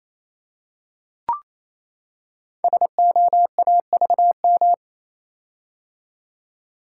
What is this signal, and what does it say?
Morse code, heard as a keyed tone, about 27 wpm, 700 Hz. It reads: SOAVM
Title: Code Groups - 5 Letters - 28wpm